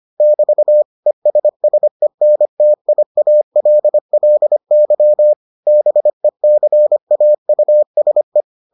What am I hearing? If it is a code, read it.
Morse code, heard as a keyed tone, about 25 wpm, 600 Hz. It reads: <BT> ESSENTIALLY BECAUSE